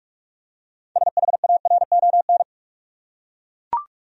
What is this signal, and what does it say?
Morse code, heard as a keyed tone, about 42 wpm, 700 Hz. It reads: SHARON
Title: USA Common Names - 45wpm